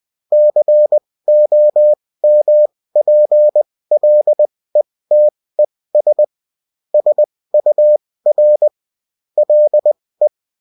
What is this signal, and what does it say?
Morse code, heard as a keyed tone, about 20 wpm, spaced as for 16 wpm, 600 Hz. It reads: COMPLETES SUR LE
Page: bulletin-uft-sous-forme-texte-et-en-cw